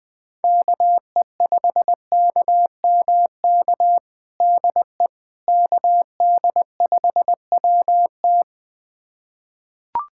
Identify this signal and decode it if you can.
Morse code, heard as a keyed tone, about 20 wpm, 700 Hz. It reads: KE5KMK DE KD5WT
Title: USA Callsign DE Callsign 20wpm